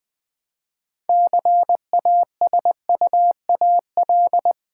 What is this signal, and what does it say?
Morse code, heard as a keyed tone, about 20 wpm, 700 Hz. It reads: CASUAL
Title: Build A Word - 2 Letters At A Time - 6-Letter Words - 20wpm